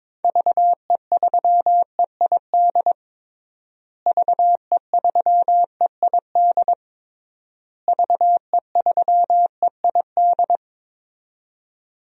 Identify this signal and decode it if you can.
Morse code, heard as a keyed tone, about 22 wpm, 700 Hz. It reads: VE3EID VE3EID VE3EID
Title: DXCC-100 Call Signs - 6 Characters - Repeated 3x - 22wpm